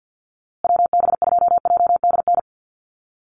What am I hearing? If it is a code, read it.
Morse code, heard as a keyed tone, about 48 wpm, 700 Hz. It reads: WB2JDN
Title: USA Call Signs Repeated 2x 50wpm